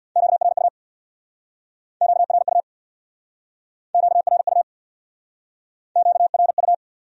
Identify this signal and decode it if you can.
Morse code, heard as a keyed tone, about 63 wpm, 700 Hz. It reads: QRU QRU QRU QRU